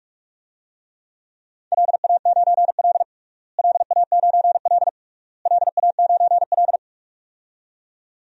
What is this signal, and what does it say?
Morse code, heard as a keyed tone, about 46 wpm, 700 Hz. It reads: LA9L LA9L LA9L
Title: DXCC-100 Call Signs - 4 Characters - Repeated 3x - 45wpm